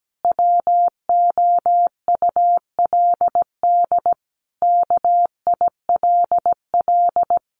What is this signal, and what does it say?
Morse code, heard as a keyed tone, about 17 wpm, 700 Hz. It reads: WOULD KILL
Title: Sentences from Top 1400 Words 17wpm